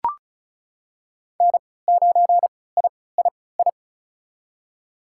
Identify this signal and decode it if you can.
Morse code, heard as a keyed tone, about 34 wpm, spaced as for 20 wpm, 700 Hz. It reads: N9III
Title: USA Call Signs - 5 Characters - 3x Character Spacing - 35wpm